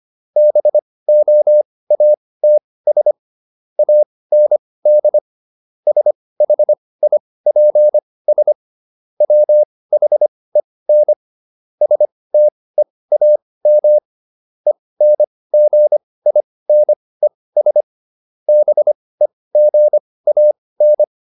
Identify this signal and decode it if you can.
Morse code, heard as a keyed tone, about 25 wpm, spaced as for 18 wpm, 600 Hz. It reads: BOATS AND SHIPS WHEN STEAM ENGINES BEGAN